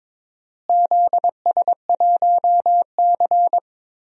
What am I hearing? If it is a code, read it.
Morse code, heard as a keyed tone, about 22 wpm, 700 Hz. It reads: ZS1C
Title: DXCC-100 Call Signs - 4 Characters - Repeated 3x - 22wpm